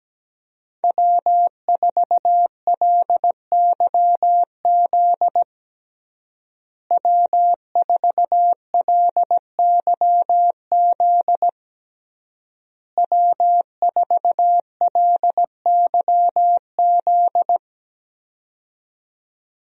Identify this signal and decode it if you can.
Morse code, heard as a keyed tone, about 17 wpm, 700 Hz. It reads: W4LYZ W4LYZ W4LYZ
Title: USA Call Signs - 5 Characters - Repeated 3x - 17wpm